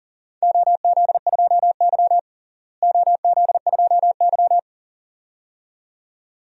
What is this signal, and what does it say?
Morse code, heard as a keyed tone, about 38 wpm, 690 Hz. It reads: OZ2Y OZ2Y
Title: DXCC-100 Call Signs - 4 Characters - Repeated 2x - 40wpm